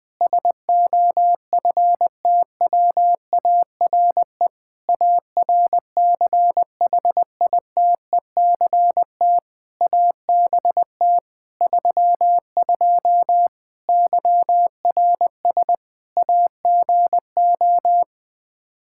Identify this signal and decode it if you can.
Morse code, heard as a keyed tone, about 21 wpm, 700 Hz. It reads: SOFTWARE ARCHITECT ABT 32 YRS AGO
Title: Practice QSO - Exchange 3 - More Info - 20wpm